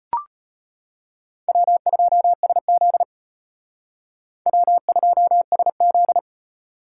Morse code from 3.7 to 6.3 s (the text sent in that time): W2SZ